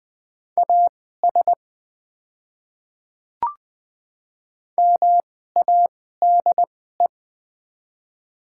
Morse code, heard as a keyed tone, about 20 wpm, spaced as for 14 wpm, 700 Hz. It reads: AS MADE
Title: 2x Character Spacing - Top 200 Words - 20wpm